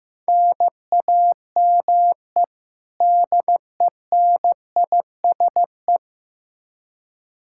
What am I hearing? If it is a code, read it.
Morse code, heard as a keyed tone, about 15 wpm, 700 Hz. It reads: NAME DENISE
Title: Practice QSO - Exchange 1 - Beginning - 15wpm